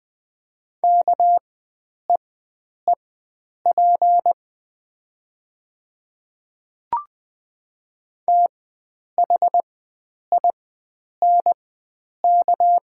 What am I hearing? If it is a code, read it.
Morse code, heard as a keyed tone, about 20 wpm, spaced as for 9 wpm, 700 Hz. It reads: KEEP THINK
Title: 4x Character Spacing - Top 300 Words - 20wpm